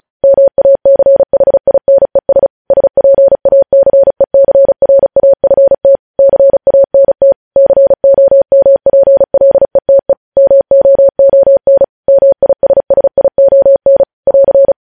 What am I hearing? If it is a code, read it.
Morse code, heard as a keyed tone, about 36 wpm, 550 Hz. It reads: MACHINES SPACECRAFT CANT COMPLETE MOON MISSION <AR>
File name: podcast / News headlines-1x-35x35